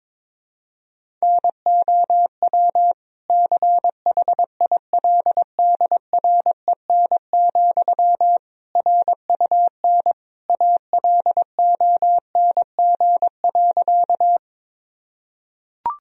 Morse code, heard as a keyed tone, about 22 wpm, 700 Hz. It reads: NOW CHILDREN, RUN ALONG.
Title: Sentences from Top 1000 Words - Review Part 1 - 22wpm